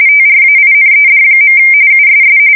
RTTY, about 45.45 baud. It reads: TRAINED. USS
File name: fsk170 / USS-New-Jersey